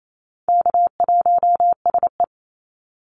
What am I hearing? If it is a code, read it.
Morse code, heard as a keyed tone, about 28 wpm, 700 Hz. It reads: K1SE
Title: Call Signs Repeated 2x - 28wpm